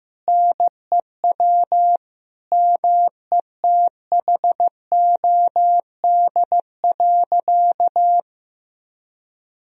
Morse code, heard as a keyed tone, about 16 wpm, 700 Hz. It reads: NEW METHOD.
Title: Sentences from Top 1000 Words - Review Part 3 - 15wpm